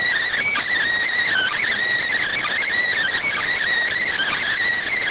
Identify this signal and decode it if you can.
Emergency Alert System (SAME) tones, heard as harsh buzzing bursts. It is heard from 1.1 to 4.6 s.